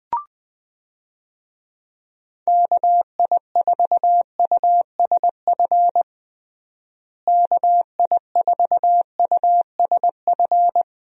Morse code, heard as a keyed tone, about 20 wpm, 700 Hz. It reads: KI4USF KI4USF
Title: USA Call Signs - 6 Characters - Repeated 2x - 20wpm